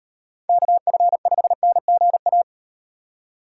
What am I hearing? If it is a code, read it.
Morse code, heard as a keyed tone, about 38 wpm, 700 Hz. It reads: KF5NGA